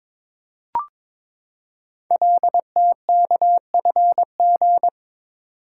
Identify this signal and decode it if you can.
Morse code, heard as a keyed tone, about 22 wpm, 700 Hz. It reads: LTKFG
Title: Code Groups - 5 Letters - 22wpm